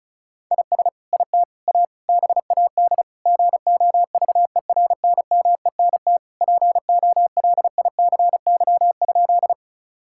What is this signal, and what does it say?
Morse code, heard as a keyed tone, about 34 wpm, 700 Hz. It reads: IS IT A BAD GOVERNMENT POLICY?
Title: Sentences from Top 1000 Words - Review Part 5 - 35wpm